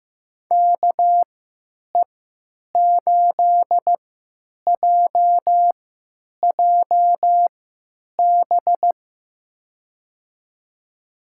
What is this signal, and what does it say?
Morse code, heard as a keyed tone, about 15 wpm, spaced as for 9 wpm, 700 Hz. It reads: KE8JJB
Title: USA Call Signs - 6 Characters - 3x Character Spacing - 15wpm